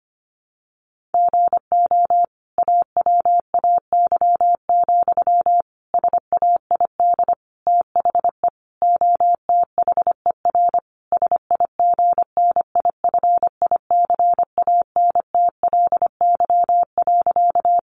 Morse code, heard as a keyed tone, about 25 wpm, 700 Hz. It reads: GO AWAY, SAID THE OTHER SIGNIFICANTLY.
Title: Sentences from Top 1500 Words 25wpm